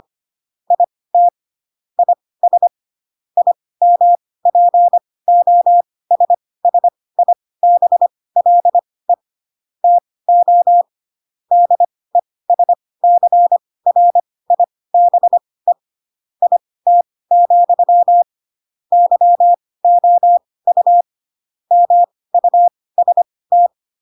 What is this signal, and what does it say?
Morse code, heard as a keyed tone, about 25 wpm, spaced as for 18 wpm, 700 Hz. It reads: IT IS IMPOSSIBLE TO DESCRIBE IT, YOU MUST
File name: Pinocchio / Pinocchio ch5 12 25 ik1hge